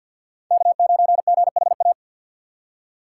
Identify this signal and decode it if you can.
Morse code, heard as a keyed tone, about 50 wpm, 690 Hz. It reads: K9GHA
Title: USA Call Signs - 5 Characters - 6x Character Spacing - 50wpm